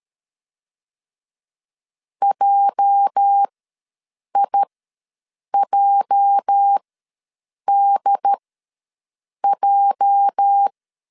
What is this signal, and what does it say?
Morse code, heard as a keyed tone, about 13 wpm, spaced as for 7 wpm, 800 Hz. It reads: JIJDJ
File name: LEZIONI / 8 YJ45 / 8 YJ ws 12wpm 3342